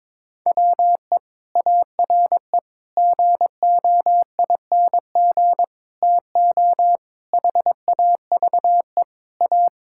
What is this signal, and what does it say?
Morse code, heard as a keyed tone, about 22 wpm, 700 Hz. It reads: WE ARE GOING TO HAVE A
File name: practice / 22wpm / Sentences with 4-Word Phrases - 22wpm